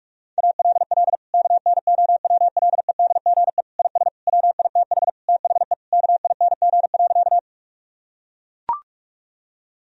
Morse code, heard as a keyed tone, about 41 wpm, 700 Hz. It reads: ALL KNOWLEDGE IS WITH THE KING.
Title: Sentences from Top 1000 Words - Review Part 3 - 45wpm